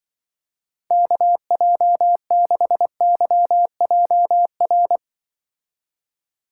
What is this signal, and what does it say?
Morse code, heard as a keyed tone, about 24 wpm, 700 Hz. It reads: KJ6YJR
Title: Speed Racing - USA 2x3 Call Signs 20wpm